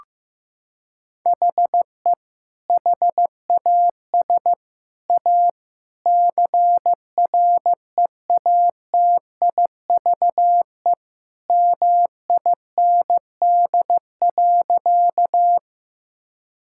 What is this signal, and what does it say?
Morse code, heard as a keyed tone, about 15 wpm, 700 Hz. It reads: HE HAS A CREATIVE MIND.
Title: Sentences from Top 1000 Words - Review Part 5 - 15wpm